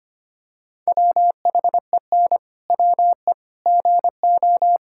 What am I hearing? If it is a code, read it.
Morse code, heard as a keyed tone, about 25 wpm, 700 Hz. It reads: WHEN WE GO
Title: Pangram Challenge v2 - 25wpm